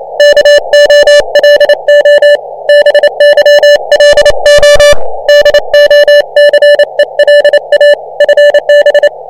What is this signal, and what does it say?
Morse code, heard as a keyed tone, about 26 wpm, 590 Hz. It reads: KOLO BYLO DOCELA FB